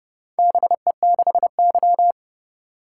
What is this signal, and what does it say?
Morse code, heard as a keyed tone, about 31 wpm, 700 Hz. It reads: BE6Y